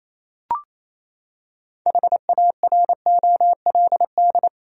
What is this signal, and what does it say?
Morse code, heard as a keyed tone, about 28 wpm, 700 Hz. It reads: HAROLD
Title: USA Common Names - 28wpm